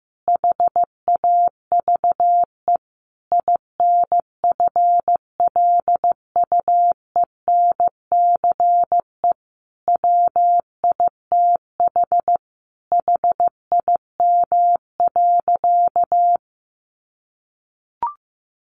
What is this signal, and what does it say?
Morse code, heard as a keyed tone, about 14 wpm, 700 Hz. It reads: HAVE INFLUENCE WITH HIM.
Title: Sentences from Top 1100 Words 15wpm